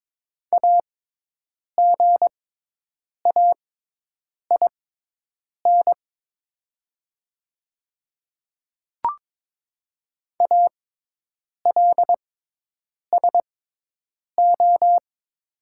Morse code, heard as a keyed tone, about 22 wpm, spaced as for 8 wpm, 700 Hz. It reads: AGAIN ALSO
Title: 6x Character Spacing - Top 500 Words - 22wpm